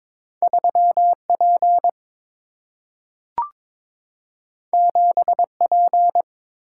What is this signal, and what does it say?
Morse code, heard as a keyed tone, about 22 wpm, 700 Hz. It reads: 3P 7P
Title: Single-Letter Single-Number 22wpm